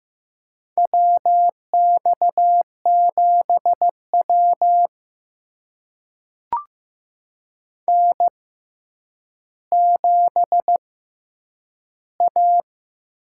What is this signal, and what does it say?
Morse code, heard as a keyed tone, about 15 wpm, spaced as for 9 wpm, 700 Hz. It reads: WX7W N7A
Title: USA Call Signs - 4 Characters - 6x Character Spacing - 15wpm